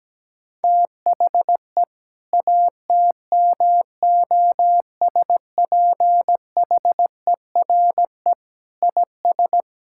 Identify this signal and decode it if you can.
Morse code, heard as a keyed tone, about 17 wpm, 700 Hz. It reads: THE ATMOSPHERE IS
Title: Sentences from Top 1600 Words 17wpm